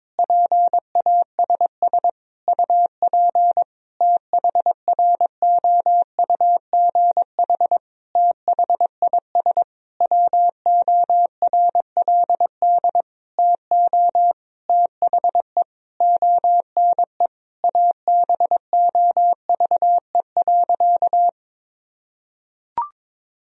Morse code, heard as a keyed tone, about 22 wpm, 700 Hz. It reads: PASS UP THROUGH THIS WORLD TO THE ONE ABOVE.
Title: Sentences from Top 1000 Words - Review Part 2 - 22wpm